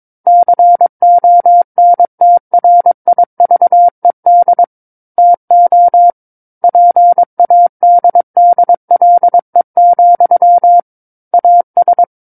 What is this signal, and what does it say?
Morse code, heard as a keyed tone, about 22 wpm, 700 Hz. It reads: CONTRIVED TO PADDLE, AS
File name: Book One / 13-Chapter 13 - 22 WPM